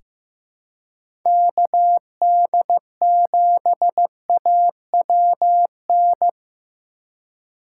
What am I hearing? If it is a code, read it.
Morse code, heard as a keyed tone, about 15 wpm, 700 Hz. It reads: KD7AWN
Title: USA Call Signs - 6 Characters - 2x Character Spacing - 15wpm